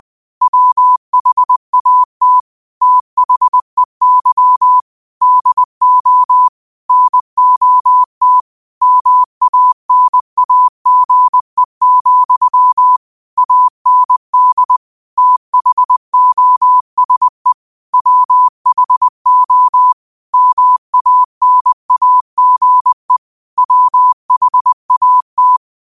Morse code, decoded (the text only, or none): WHAT THEY DO NOT MANAGE, AND THOSE WHO MANAGE WHAT